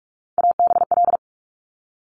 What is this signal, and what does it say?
Morse code, heard as a keyed tone, about 46 wpm, 700 Hz. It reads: ABL
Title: Top Three Letter Combinations in Words 45wpm